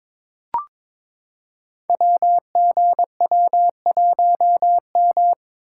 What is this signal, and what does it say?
Morse code, heard as a keyed tone, about 22 wpm, 700 Hz. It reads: WGW1M
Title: Code Groups - 5 Letter:Numbers - 22wpm